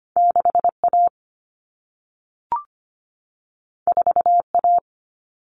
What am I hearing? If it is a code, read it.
Morse code, heard as a keyed tone, about 25 wpm, 700 Hz. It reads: 6A 4A
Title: Common Cut Numbers - 2-Digit 25wpm